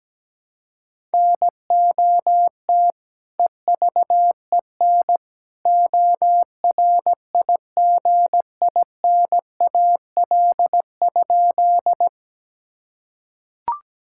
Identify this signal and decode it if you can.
Morse code, heard as a keyed tone, about 17 wpm, 700 Hz. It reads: NOT EVEN ORIGINAL?
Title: Sentences from Top 1000 Words - Review Part 6 - 17wpm